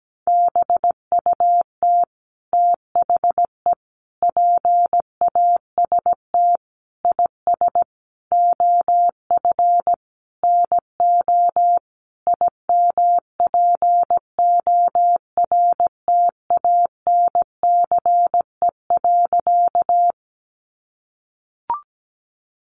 Morse code, heard as a keyed tone, about 17 wpm, 700 Hz. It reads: BUT THE PAST IS OF NO IMPORTANCE.
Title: Sentences from Top 1200 Words 17wpm